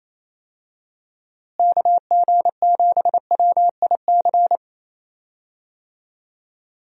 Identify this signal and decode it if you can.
Morse code, heard as a keyed tone, about 28 wpm, 700 Hz. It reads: KG7WIC
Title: USA Call Signs - 6 Characters - Repeated 2x - 28wpm